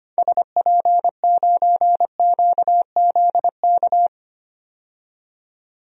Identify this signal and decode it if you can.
Morse code, heard as a keyed tone, about 25 wpm, 700 Hz. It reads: SP9QZK
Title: DXCC-100 Call Signs - 6 Characters - 25wpm